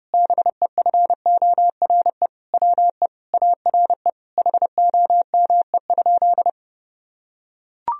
Morse code, heard as a keyed tone, about 30 wpm, 700 Hz. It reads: BEFORE WE ARE HOME?
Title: Sentences from Top 1000 Words - Review Part 1 - 30wpm